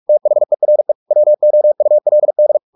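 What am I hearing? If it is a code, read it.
Morse code, heard as a keyed tone, about 44 wpm, 600 Hz. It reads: THERE WOULD
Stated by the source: Morse practice at 45 wpm, the novel Never by Ken Follett in English, from CW med Gnister.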